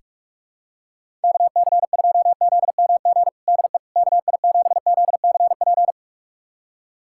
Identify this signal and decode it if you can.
Morse code, heard as a keyed tone, about 46 wpm, 700 Hz. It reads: KC2ZMG DE KI7ZCP
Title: USA Callsign DE Callsign - 6 Characters - 45wpm